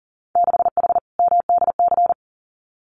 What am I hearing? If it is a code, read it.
Morse code, heard as a keyed tone, about 40 wpm, 700 Hz. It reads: 6H MDC
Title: Winter Field Day Exchanges Encore 40wpm